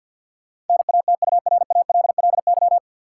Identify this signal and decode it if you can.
Morse code, heard as a keyed tone, about 48 wpm, 700 Hz. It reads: NATURALLY
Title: Sentences from Top 1300 Words - New Word Drill Alt - 50wpm